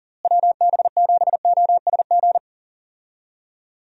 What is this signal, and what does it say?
Morse code, heard as a keyed tone, about 37 wpm, 700 Hz. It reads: WB7OSG
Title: USA Call Signs - 6 Characters - Repeated 3x - 40wpm